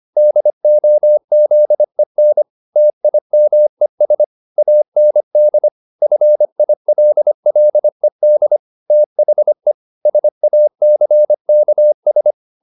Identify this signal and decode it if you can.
Morse code, heard as a keyed tone, about 25 wpm, 600 Hz. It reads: DOZEN TIMES AND FILLED THE SACKS